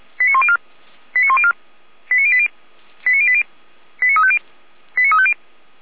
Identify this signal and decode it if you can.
Five-tone selective call (ZVEI), heard as quick runs of five tones, tones of about 70 ms. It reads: 89104 89104 89080 89080 89350 89350
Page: ZVEI